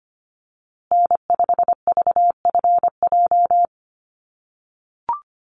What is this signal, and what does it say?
Morse code, heard as a keyed tone, about 25 wpm, 700 Hz. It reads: N5VFJ
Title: USA Call Signs Repeated 3x 25wpm